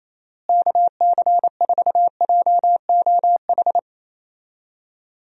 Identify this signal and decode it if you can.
Morse code, heard as a keyed tone, about 28 wpm, 700 Hz. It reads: KC4JOH
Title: USA Call Signs - 6 Characters - Repeated 2x - 28wpm